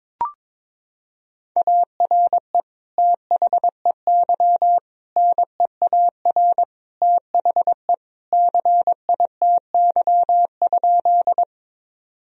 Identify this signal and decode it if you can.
Morse code, heard as a keyed tone, about 22 wpm, 700 Hz. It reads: ARE THEY NEAR THE CITY?
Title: Sentences from Top 1000 Words - Review Part 1 - 22wpm